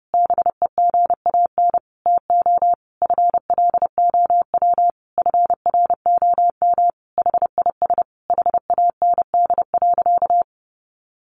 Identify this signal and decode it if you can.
Morse code, heard as a keyed tone, about 28 wpm, 700 Hz. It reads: BEGAN TO FLOW FROM HIS HAND.
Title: Sentences from Top 1200 Words 30wpm